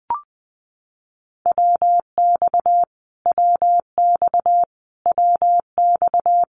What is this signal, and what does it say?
Morse code, heard as a keyed tone, about 20 wpm, 700 Hz. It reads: WX WX WX
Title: Abbreviations Repeated 3x - 20wpm